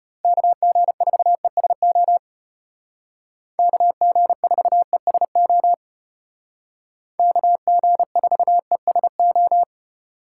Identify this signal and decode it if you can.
Morse code, heard as a keyed tone, about 32 wpm, 700 Hz. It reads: KG4ESO KG4ESO KG4ESO